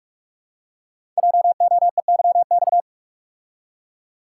Morse code, heard as a keyed tone, about 44 wpm, 690 Hz. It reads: JOEYX
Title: Code Groups - 5 Letter:Numbers - 45wpm